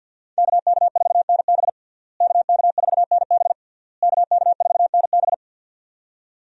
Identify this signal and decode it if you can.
Morse code, heard as a keyed tone, about 52 wpm, 700 Hz. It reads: KK4NB KK4NB KK4NB
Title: USA Call Signs - 5 Characters - Repeated 3x - 50wpm